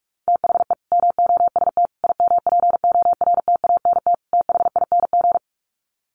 Morse code, heard as a keyed tone, about 44 wpm, 700 Hz. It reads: THE MOST IMPORTANT THING
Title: Sets of 4 Words 45wpm